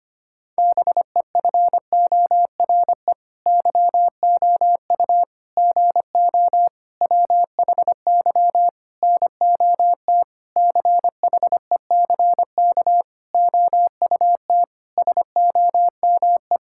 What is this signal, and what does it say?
Morse code, heard as a keyed tone, about 25 wpm, 700 Hz. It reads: BEFORE YOU GO WHY NOT CHECK OUT SOME
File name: Sets of 9 Words - 25wpm